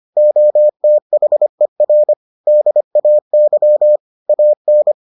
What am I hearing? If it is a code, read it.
Morse code, heard as a keyed tone, about 25 wpm, 600 Hz. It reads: OTHER DAY AN